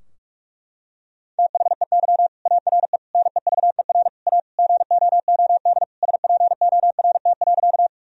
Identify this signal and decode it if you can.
Morse code, heard as a keyed tone, about 46 wpm, 700 Hz. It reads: THEY ARE NEVER A GOOD SPORT.
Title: Sentences from Top 1000 Words - Review Part 4 - 45wpm